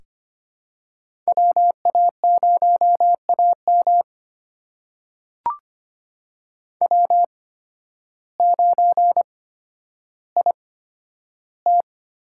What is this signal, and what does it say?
Morse code, heard as a keyed tone, about 25 wpm, spaced as for 12 wpm, 700 Hz. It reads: WA0AM W9IT
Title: USA Call Signs - 5 Characters - 8x Character Spacing - 25wpm